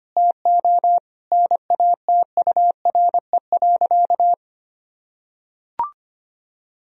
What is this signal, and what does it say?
Morse code, heard as a keyed tone, about 25 wpm, 700 Hz. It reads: TO NATURE.
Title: Sentences from Top 1000 Words - Review Part 4 - 25wpm